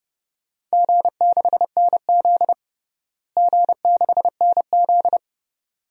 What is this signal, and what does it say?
Morse code, heard as a keyed tone, about 32 wpm, 700 Hz. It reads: G6NZ G6NZ
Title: DXCC-100 Call Signs - 4 Characters - Repeated 2x - 30wpm